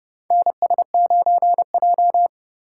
Morse code, heard as a keyed tone, about 30 wpm, 700 Hz. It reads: NS9J
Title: USA Call Signs - 4x Character Spacing - 30wpm